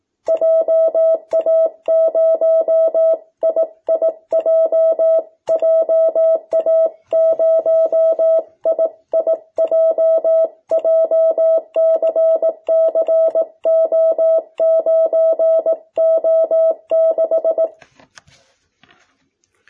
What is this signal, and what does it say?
Morse code, heard as a keyed tone, about 18 wpm, 660 Hz. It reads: JA0IIJ JA0IIJ JCCO9O6
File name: jh8cbh / 2021cw